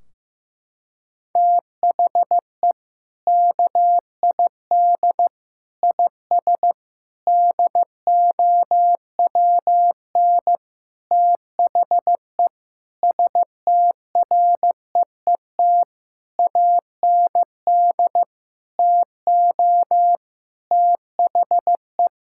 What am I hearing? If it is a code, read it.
Morse code, heard as a keyed tone, about 15 wpm, 700 Hz. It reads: THE KID IS DOWN THE STREET AND TO THE
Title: Sentences from Top 1600 Words 15wpm